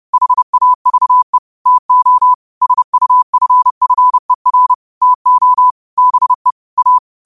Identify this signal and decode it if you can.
Morse code, heard as a keyed tone, about 30 wpm, 1000 Hz. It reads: HAVE TO SUFFER TO BE A
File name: qotdfiles / 30wpm